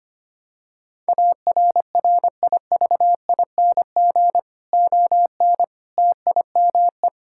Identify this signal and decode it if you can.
Morse code, heard as a keyed tone, about 25 wpm, 700 Hz. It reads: ARRIVING ON TIME
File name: practice / 25wpm / Difficult Sentences with Hints - 25wpm